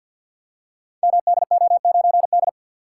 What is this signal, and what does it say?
Morse code, heard as a keyed tone, about 46 wpm, 700 Hz. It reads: MDO9D
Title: Code Groups - 5 Letter:Numbers - 50wpm